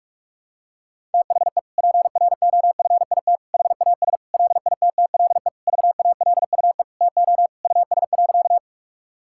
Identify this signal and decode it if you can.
Morse code, heard as a keyed tone, about 41 wpm, 700 Hz. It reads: THE PROFIT HAS LITTLE VALUE TO US.
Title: Sentences from Top 1000 Words - Review Part 5 - 45wpm